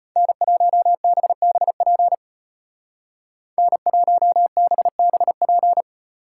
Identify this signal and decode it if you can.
Morse code, heard as a keyed tone, about 35 wpm, 700 Hz. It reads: N1BBP N1BBP